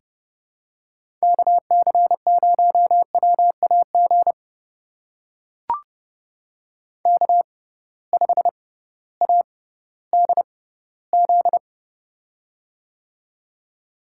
Morse code, heard as a keyed tone, about 29 wpm, spaced as for 16 wpm, 700 Hz. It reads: KC0WAG K5ADZ
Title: USA Call Signs - 6x Character Spacing - 30wpm